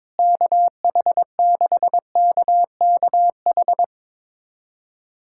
Morse code, heard as a keyed tone, about 22 wpm, 700 Hz. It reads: KH6KKH
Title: USA Callsign DE Callsign 22wpm